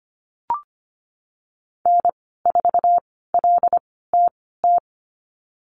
Morse code, heard as a keyed tone, about 25 wpm, spaced as for 16 wpm, 700 Hz. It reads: N4LTT
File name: Lesson 15g - Number 5 - Call Signs 25wpm